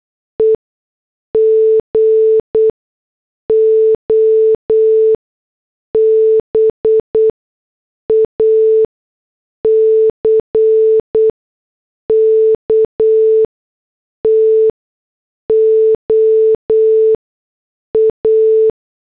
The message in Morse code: E GO BACK TO A